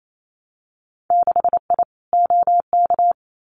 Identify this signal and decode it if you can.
Morse code, heard as a keyed tone, about 28 wpm, 700 Hz. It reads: 6I OK
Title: Winter Field Day Exchanges Encore 28wpm